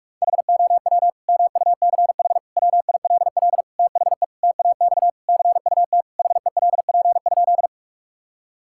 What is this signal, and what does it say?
Morse code, heard as a keyed tone, about 46 wpm, 700 Hz. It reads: HOW MUCH WILL THE TAX CUT HELP?
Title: Sentences from Top 1000 Words - Review Part 5 - 45wpm